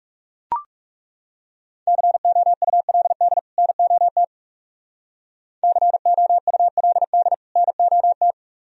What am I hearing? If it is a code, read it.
Morse code, heard as a keyed tone, about 36 wpm, 700 Hz. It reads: COULD NOT COULD NOT
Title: Speed-Racing - Sets of 2 Words 30wpm